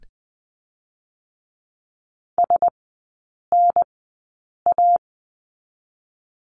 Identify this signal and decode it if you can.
Morse code, heard as a keyed tone, about 20 wpm, 700 Hz. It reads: S N A
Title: 035 Lesson 7e - Letter s - Three Characters 20wpm